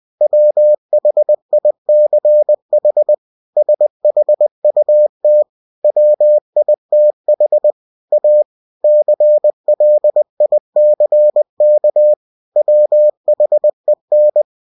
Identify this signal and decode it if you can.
Morse code, heard as a keyed tone, about 20 wpm, 600 Hz. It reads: WHICH SHUT WITH A CLICK WHEN
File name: cw-books / winnie-the-pooh / winnie-20wpm-page-0109